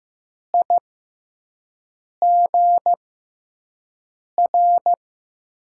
Morse code, heard as a keyed tone, about 15 wpm, spaced as for 5 wpm, 700 Hz. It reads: IGR
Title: USA Call Signs - 6 Characters - 6x Character Spacing - 15wpm